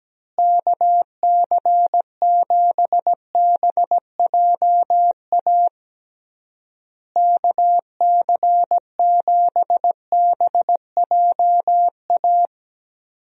Morse code, heard as a keyed tone, about 17 wpm, 700 Hz. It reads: KC7BJA KC7BJA
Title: USA Call Signs - 6 Characters - Repeated 3x - 17wpm